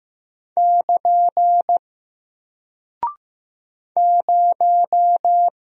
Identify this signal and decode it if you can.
Morse code, heard as a keyed tone, about 15 wpm, 700 Hz. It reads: <KN> 0
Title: Single Character (Letter:Number:Punctuation:Prosign) Alt 15wpm